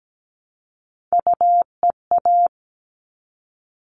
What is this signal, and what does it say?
Morse code, heard as a keyed tone, about 17 wpm, 700 Hz. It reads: UEA
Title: All Cut Numbers - 3-Digit 17wpm